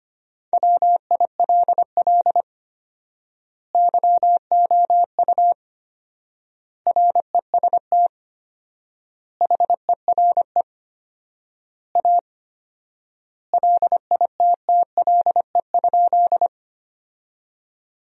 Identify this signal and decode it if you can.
Morse code, heard as a keyed tone, about 25 wpm, 700 Hz. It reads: WILL YOU REST HERE A LITTLE?
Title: 4x Word-Spacing - Sentences from Top 500 Words - 25wpm